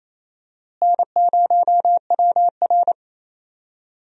Morse code, heard as a keyed tone, about 28 wpm, 700 Hz. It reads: N0WR
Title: USA Call Signs - 4 Characters - Repeated 2x - 28wpm